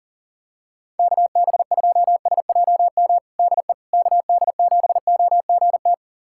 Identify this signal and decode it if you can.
Morse code, heard as a keyed tone, about 37 wpm, 700 Hz. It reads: KB2SJM DE KD7OGT
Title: USA Callsign DE Callsign - 6 Characters - 40wpm